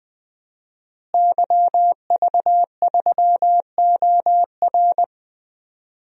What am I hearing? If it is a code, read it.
Morse code, heard as a keyed tone, about 20 wpm, 700 Hz. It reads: YV3OR
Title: DXCC-100 Call Signs - 5 Characters - Repeated 3x - 20wpm